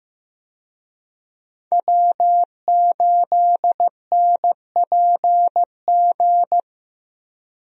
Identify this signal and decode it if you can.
Morse code, heard as a keyed tone, about 15 wpm, 700 Hz. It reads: W8NPG
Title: USA Call Signs - 5 Characters - Repeated 3x - 15wpm